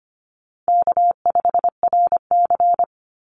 Morse code, heard as a keyed tone, about 25 wpm, 700 Hz. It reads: K5RC
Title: Call Signs Repeated 3x - 25wpm